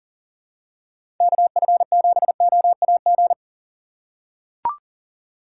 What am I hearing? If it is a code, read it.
Morse code, heard as a keyed tone, about 40 wpm, 700 Hz. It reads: KF7OAG
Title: USA Call Signs - 6 Characters - 2x Character Spacing - 40wpm